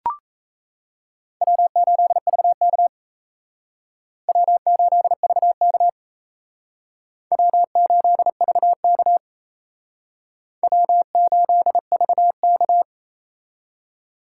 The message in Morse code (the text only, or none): W8VK W8VK W8VK W8VK